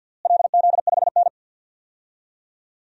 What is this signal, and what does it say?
Morse code, heard as a keyed tone, about 51 wpm, 700 Hz. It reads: LZ5N
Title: DXCC-100 Call Signs - 4 Characters - Repeated 2x - 50wpm